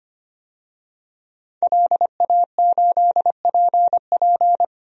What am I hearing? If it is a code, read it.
Morse code, heard as a keyed tone, about 25 wpm, 700 Hz. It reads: LA8PP
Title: DXCC-100 Call Signs - 5 Characters - 25wpm